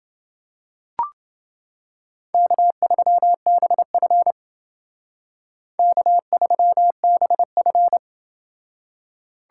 Morse code, heard as a keyed tone, about 30 wpm, 700 Hz. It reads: K3BF K3BF